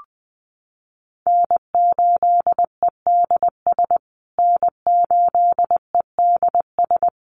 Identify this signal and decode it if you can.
Morse code, heard as a keyed tone, about 20 wpm, 700 Hz. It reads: N8EDS N8EDS
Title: USA Call Signs Repeated 2x 20wpm